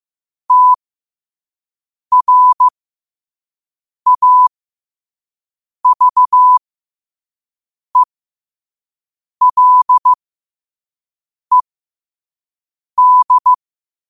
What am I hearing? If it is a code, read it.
Morse code, heard as a keyed tone, about 15 wpm, spaced as for 5 wpm, 1000 Hz. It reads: TRAVELED
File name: qotdfiles / 5wpm